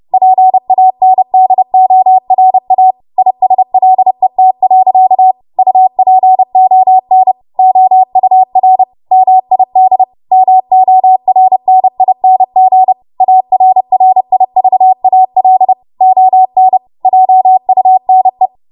Morse code, heard as a keyed tone, about 30 wpm, 750 Hz. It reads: PANDORA ISLET. UPON OUR MID MORNING ARRIVAL ON JUNE
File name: Morse / 30WPM